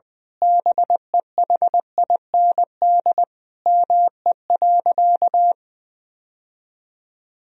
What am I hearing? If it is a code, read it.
Morse code, heard as a keyed tone, about 20 wpm, 700 Hz. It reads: BEHIND ME.
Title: Sentences from Top 1000 Words - Review Part 1 - 20wpm